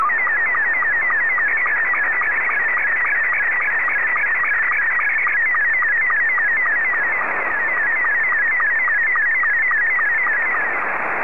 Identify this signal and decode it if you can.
SSTV (slow-scan television), Scottie 2.